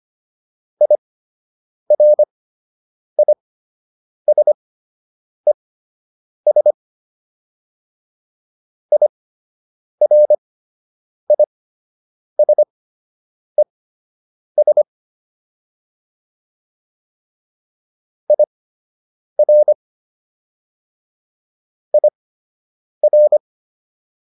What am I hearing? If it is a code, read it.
Morse code, heard as a keyed tone, about 25 wpm, spaced as for 8 wpm, 600 Hz. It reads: IRISES IRISES IR IR